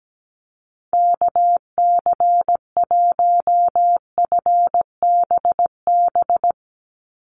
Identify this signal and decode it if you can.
Morse code, heard as a keyed tone, about 17 wpm, 700 Hz. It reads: KC1FBB
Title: USA Call Signs 17wpm